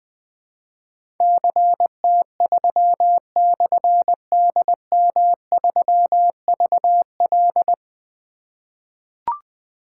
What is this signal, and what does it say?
Morse code, heard as a keyed tone, about 20 wpm, 700 Hz. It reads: CT3/DM3VL
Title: Difficult Call Signs 20wpm